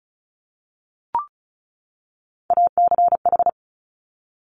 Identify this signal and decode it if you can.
Morse code, heard as a keyed tone, about 36 wpm, 700 Hz. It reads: ACH
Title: Top Three Letter Combinations in Words 35wpm